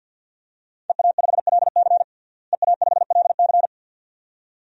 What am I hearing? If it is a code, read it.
Morse code, heard as a keyed tone, about 51 wpm, 700 Hz. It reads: EA5LC EA5LC
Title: DXCC-100 Call Signs - 5 Characters - Repeated 2x - 50wpm